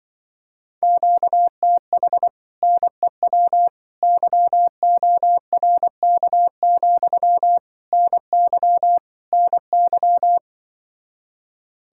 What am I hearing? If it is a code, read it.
Morse code, heard as a keyed tone, about 24 wpm, 700 Hz. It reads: QTH NEW YORK, NY NY
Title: Speed-Racing Common QSO Phrases - 20wpm